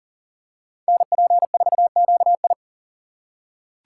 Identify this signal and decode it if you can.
Morse code, heard as a keyed tone, about 38 wpm, 700 Hz. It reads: NP4QI
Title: USA Call Signs - 5 Characters - 3x Character Spacing - 40wpm